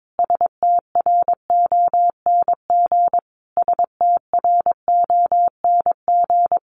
Morse code, heard as a keyed tone, about 22 wpm, 700 Hz. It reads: STRONG STRONG
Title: Top 500 Words Repeated 22wpm